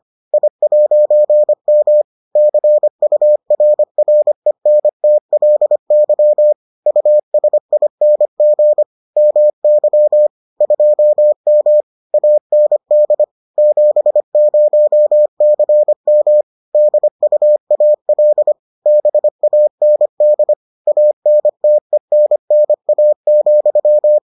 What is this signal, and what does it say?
Morse code, heard as a keyed tone, about 25 wpm, 600 Hz. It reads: I'M CURRENTLY USING MY 2M AND 70CM DUAL BAND ANTENNA,